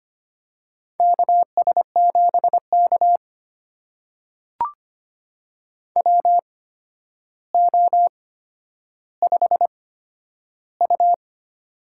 Morse code, heard as a keyed tone, about 25 wpm, spaced as for 11 wpm, 700 Hz. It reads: KS7K WO5U
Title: USA Call Signs - 4 Characters - 8x Character Spacing - 25wpm